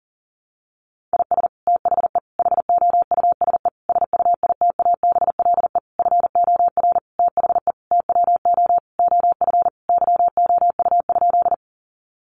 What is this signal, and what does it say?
Morse code, heard as a keyed tone, about 40 wpm, 700 Hz. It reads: IS THE HOUSE SUITABLE FOR THE TWO OF YOU?
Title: Sentences from Top 1400 Words 40wpm